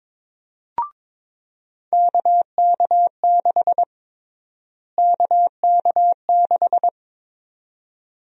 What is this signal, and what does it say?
Morse code, heard as a keyed tone, about 22 wpm, 700 Hz. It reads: KK6 KK6
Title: USA Call Sign Prefixes - Repeated 2x - 22wpm